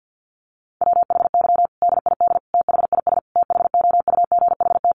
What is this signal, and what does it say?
Morse code, heard as a keyed tone, about 50 wpm, 700 Hz. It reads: WHY DID THIS THOUGHT
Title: Sentences from Top 900 Words 50wpm